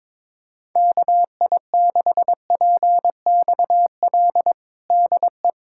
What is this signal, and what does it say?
Morse code, heard as a keyed tone, about 22 wpm, 700 Hz. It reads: KI6PXL DE
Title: USA Callsign DE Callsign 22wpm